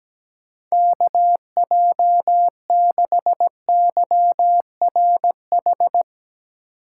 Morse code, heard as a keyed tone, about 17 wpm, 700 Hz. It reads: KJ6YRH
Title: USA Call Signs - 6 Characters - 2x Character Spacing - 17wpm